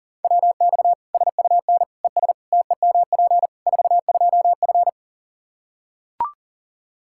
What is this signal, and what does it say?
Morse code, heard as a keyed tone, about 40 wpm, 700 Hz. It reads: WX SUN ES TEMP 42F
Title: K1USN SST QSO Practice 40wpm